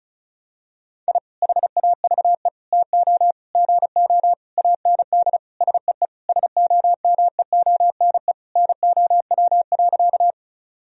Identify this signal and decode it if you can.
Morse code, heard as a keyed tone, about 36 wpm, 700 Hz. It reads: I HAVE TO GO AND SEE SOMEONE NOW.
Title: Sentences from Top 1000 Words - Review Part 4 - 35wpm